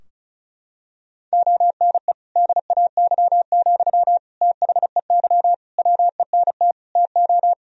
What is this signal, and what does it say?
Morse code, heard as a keyed tone, about 36 wpm, 700 Hz. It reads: ONE DAY, THEY WENT TO
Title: Sentences from Top 1000 Words - Review Part 2 - 35wpm